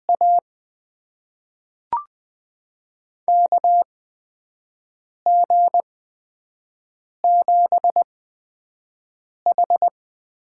Morse code, heard as a keyed tone, about 20 wpm, spaced as for 5 wpm, 700 Hz. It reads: A KG7H
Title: USA Call Signs - 6 Characters - 8x Character Spacing - 20wpm